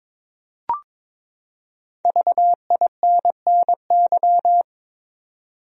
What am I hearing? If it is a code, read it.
Morse code, heard as a keyed tone, about 22 wpm, 700 Hz. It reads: VINNY
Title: USA Common Names - 22wpm